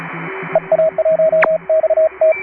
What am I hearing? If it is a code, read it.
Morse code, heard as a keyed tone, about 31 wpm, 620 Hz, with radio noise. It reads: EA1XT